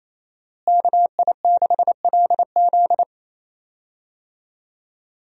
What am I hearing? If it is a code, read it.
Morse code, heard as a keyed tone, about 28 wpm, 700 Hz. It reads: KI6LZ
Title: USA Call Signs - 5 Characters - Repeated 3x - 28wpm